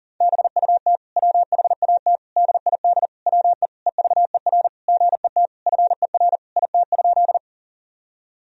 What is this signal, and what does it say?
Morse code, heard as a keyed tone, about 40 wpm, 700 Hz. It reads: BUT WHAT DID WE EVER GET FER IT?
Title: Sentences from Top 400 Words and Abbreviations 40wpm